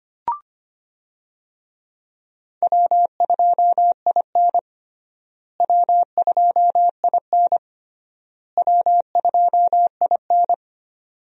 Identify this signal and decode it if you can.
Morse code, heard as a keyed tone, about 25 wpm, 700 Hz. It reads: W2IN W2IN W2IN
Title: USA Call Signs - 4 Characters - Repeated 3x - 25wpm